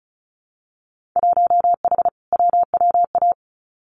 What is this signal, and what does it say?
Morse code, heard as a keyed tone, about 36 wpm, 700 Hz. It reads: 1H WWA
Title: Winter Field Day Exchanges 35wpm